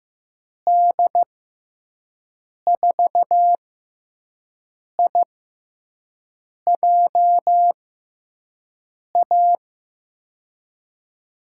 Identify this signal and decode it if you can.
Morse code, heard as a keyed tone, about 15 wpm, spaced as for 5 wpm, 700 Hz. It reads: D4IJA
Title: USA Call Signs - 6 Characters - 6x Character Spacing - 15wpm